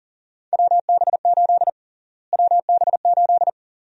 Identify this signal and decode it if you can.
Morse code, heard as a keyed tone, about 40 wpm, 700 Hz. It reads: WB8 WB8
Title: USA Call Sign Prefixes - Repeated 2x - 40wpm